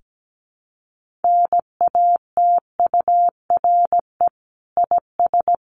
Morse code, heard as a keyed tone, about 17 wpm, 700 Hz. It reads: NATURE IS
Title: Sentences from Top 1500 Words 17wpm